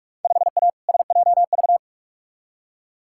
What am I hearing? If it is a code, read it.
Morse code, heard as a keyed tone, about 46 wpm, 700 Hz. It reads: 5A SJV
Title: Field Day Exchanges - 45wpm